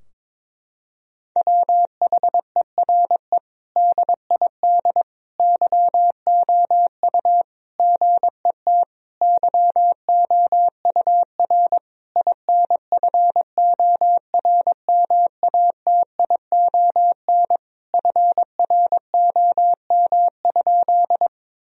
Morse code, heard as a keyed tone, about 22 wpm, 700 Hz. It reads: WHERE DID YOU GET YOUR INFORMATION FROM?
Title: Sentences from Top 1000 Words - Review Part 3 - 22wpm